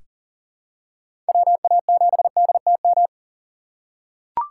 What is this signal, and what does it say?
Morse code, heard as a keyed tone, about 40 wpm, 700 Hz. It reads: WA7DTM
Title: USA Call Signs - 6 Characters - 6x Character Spacing - 40wpm